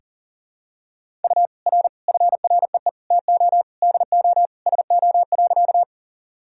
Morse code, heard as a keyed tone, about 40 wpm, 700 Hz. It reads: U R FREE TO DO SO.
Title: Sentences from Top 500 Words and Abbreviations 40wpm